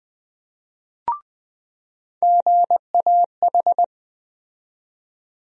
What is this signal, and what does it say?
Morse code, heard as a keyed tone, about 20 wpm, 700 Hz. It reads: GAH
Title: USA Call Sign Postfixes - 3 Letters 20wpm